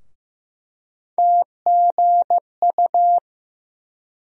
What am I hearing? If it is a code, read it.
Morse code, heard as a keyed tone, about 15 wpm, 700 Hz. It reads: TGU
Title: USA Call Sign Postfixes - 3 Letters 15wpm